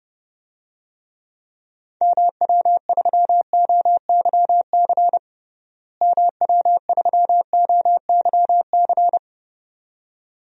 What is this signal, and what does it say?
Morse code, heard as a keyed tone, about 31 wpm, 700 Hz. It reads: MW3OYC MW3OYC
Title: DXCC-100 Call Signs - 6 Characters - Repeated 2x - 30wpm